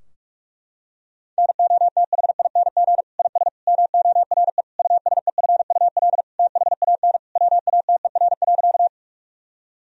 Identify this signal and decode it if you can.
Morse code, heard as a keyed tone, about 46 wpm, 700 Hz. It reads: NOTHING IS MORE USEFUL THAN WATER.
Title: Sentences from Top 1000 Words - Review Part 4 - 45wpm